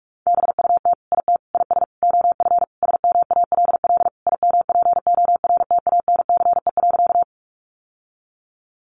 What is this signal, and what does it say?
Morse code, heard as a keyed tone, about 44 wpm, 700 Hz. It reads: BUT IT IS OF SMALL IMPORTANCE.
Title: Sentences from Top 1200 Words 45wpm